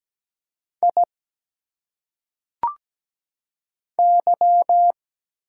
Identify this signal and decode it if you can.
Morse code, heard as a keyed tone, about 17 wpm, 700 Hz. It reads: I Y
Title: Single Letters - Phonetic Pronunciation 17wpm